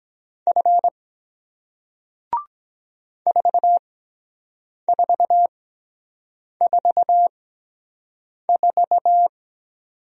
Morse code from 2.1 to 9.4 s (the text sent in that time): 4444